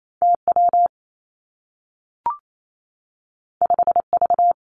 Morse code, heard as a keyed tone, about 28 wpm, 700 Hz. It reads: T W 5 V
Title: Two Challenging Letter-Number Combinations 28wpm